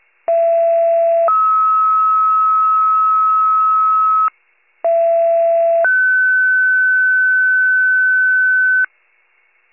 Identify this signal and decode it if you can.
Two-tone page, heard as a one-second tone, then a three-second tone, about 670 then 1290 Hz, about 670 then 1500 Hz, with noise. It is heard from 0.3 to 8.8 s.